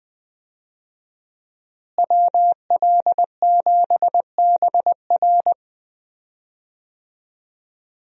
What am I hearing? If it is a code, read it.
Morse code, heard as a keyed tone, about 20 wpm, 700 Hz. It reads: WL7BR
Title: USA Call Signs - 5 Characters - Repeated 3x - 20wpm